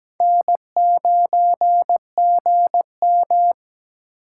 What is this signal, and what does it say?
Morse code, heard as a keyed tone, about 20 wpm, 700 Hz. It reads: N9GM
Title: Speed Racing - USA 1x2 Call Signs 17wpm